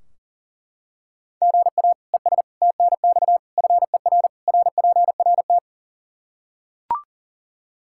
Morse code, heard as a keyed tone, about 40 wpm, 700 Hz. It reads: GA ES TNX FER RPRT
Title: Practice QSO - Exchange 1 - Beginning - 40wpm